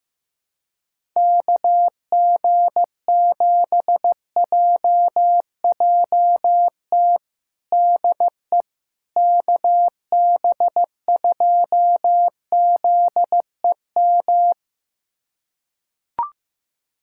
Morse code, heard as a keyed tone, about 15 wpm, 700 Hz. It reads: KG7JJT DE KB2ZEM
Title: USA Callsign DE Callsign - 6 Characters - 15wpm